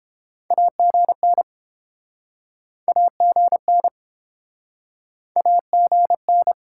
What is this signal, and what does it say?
Morse code, heard as a keyed tone, about 30 wpm, 700 Hz. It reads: AGN AGN AGN